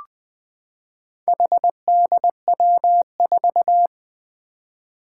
Morse code, heard as a keyed tone, about 20 wpm, 700 Hz. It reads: HDW4
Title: Four Challenging Letter-Number Combinations 20wpm